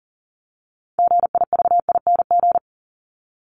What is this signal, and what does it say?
Morse code, heard as a keyed tone, about 40 wpm, 700 Hz. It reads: GIVING
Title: ING Suffix - Top 100 Words with Base Words 40wpm